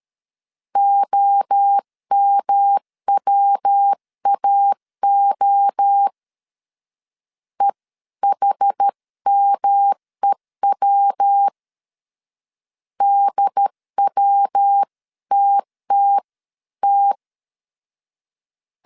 Morse code, heard as a keyed tone, about 13 wpm, 800 Hz. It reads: OMWAO EHMEW DWTTT
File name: CW / DET / 1 3 1500C S A 12wpm 2905